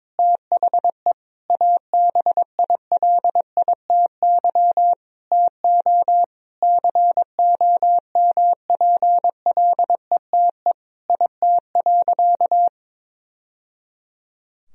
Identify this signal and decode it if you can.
Morse code, heard as a keyed tone, about 22 wpm, 700 Hz. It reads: THE ABILITY TO COMPLETE IT.
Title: Sentences from Top 1000 Words - Review Part 3 - 22wpm